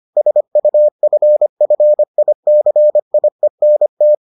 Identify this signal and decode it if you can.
Morse code, heard as a keyed tone, about 25 wpm, 600 Hz. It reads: SUFFICIENT